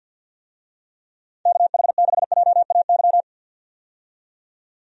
Morse code, heard as a keyed tone, about 52 wpm, 700 Hz. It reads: KH6JAY
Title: DXCC-100 Call Signs - 6 Characters - 50wpm